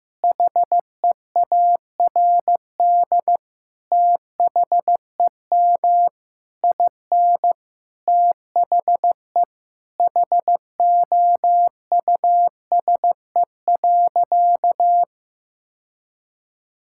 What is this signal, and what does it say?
Morse code, heard as a keyed tone, about 15 wpm, 700 Hz. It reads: HEARD THEM IN THE HOUSE.
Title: Sentences from Top 1000 Words - Review Part 5 - 15wpm